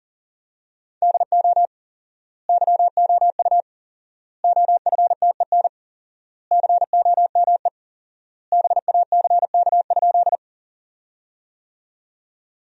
Morse code, heard as a keyed tone, about 37 wpm, 700 Hz. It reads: DO YOU OFTEN COME BACK?
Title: 4x Word-Spacing - Sentences from Top 300 Words - 40wpm